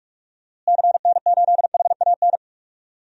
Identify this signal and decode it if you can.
Morse code, heard as a keyed tone, about 46 wpm, 700 Hz. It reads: CN8HAN
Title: DXCC-100 Call Signs - 6 Characters - Repeated 2x - 45wpm